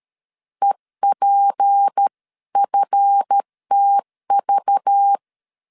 Morse code, heard as a keyed tone, about 13 wpm, 800 Hz. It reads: EPFTV
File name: LEZIONI / 7 GF9Z / 7 GF 12wpm 2122